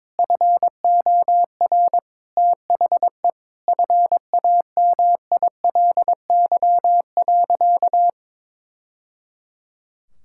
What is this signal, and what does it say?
Morse code, heard as a keyed tone, about 22 wpm, 700 Hz. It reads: FOR THE FAMILY.
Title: Sentences from Top 1000 Words - Review Part 5 - 22wpm